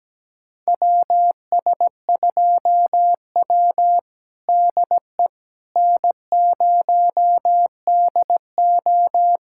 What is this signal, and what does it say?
Morse code, heard as a keyed tone, about 17 wpm, 700 Hz. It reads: WS2W DE N0DO
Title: USA Callsign DE Callsign - 4 Characters - 17wpm